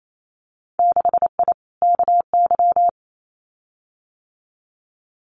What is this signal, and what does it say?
Morse code, heard as a keyed tone, about 28 wpm, 700 Hz. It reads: 6I KY
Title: Winter Field Day Exchanges 28wpm